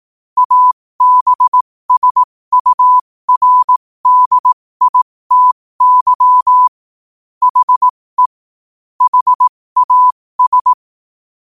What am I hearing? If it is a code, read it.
Morse code, heard as a keyed tone, about 18 wpm, spaced as for 15 wpm, 1000 Hz. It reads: ABSURDITY HE HAS
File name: qotdfiles / 15wpm